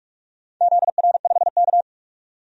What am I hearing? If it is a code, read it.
Morse code, heard as a keyed tone, about 46 wpm, 700 Hz. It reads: ZR5K
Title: DXCC-100 Call Signs - 4 Characters - 45wpm